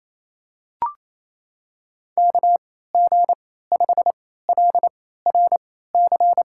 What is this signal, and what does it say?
Morse code, heard as a keyed tone, about 28 wpm, spaced as for 16 wpm, 700 Hz. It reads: KG5LRC
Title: USA Call Signs - 6 Characters - 3x Character Spacing - 28wpm